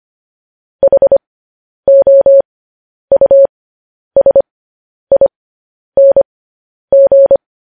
Morse code, heard as a keyed tone, about 25 wpm, spaced as for 10 wpm, 550 Hz. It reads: HOUSING
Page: News Headlines in Morse Code at 10 WPM